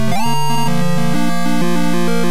There is no code.